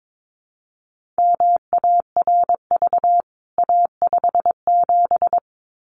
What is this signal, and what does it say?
Morse code, heard as a keyed tone, about 22 wpm, 700 Hz. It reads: MARV A57
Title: CWT Exchange 22wpm